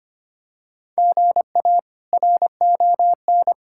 Morse code, heard as a keyed tone, about 25 wpm, 700 Hz. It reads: GA RON
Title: K1USN SST QSO Practice - 25wpm